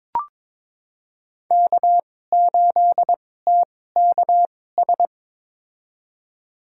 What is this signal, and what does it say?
Morse code, heard as a keyed tone, about 22 wpm, spaced as for 16 wpm, 700 Hz. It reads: K8TKS
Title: USA Call Signs - 5 Characters - 2x Character Spacing - 22wpm